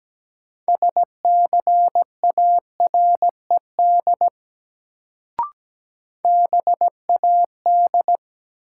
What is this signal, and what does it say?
Morse code, heard as a keyed tone, about 17 wpm, 700 Hz. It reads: SCARED BAD
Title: Sentences from Top 700 Words - New Word Drill - 17wpm